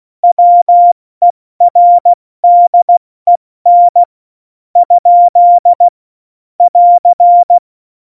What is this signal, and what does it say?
Morse code, heard as a keyed tone, about 16 wpm, spaced as for 13 wpm, 700 Hz. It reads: WERDEN? <AR>
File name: Morsekurs / Witze / Witz 17.Dezember 2019, 70 F80